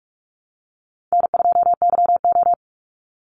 Morse code, heard as a keyed tone, about 46 wpm, 700 Hz. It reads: N2YO
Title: Easy Call Signs - 45wpm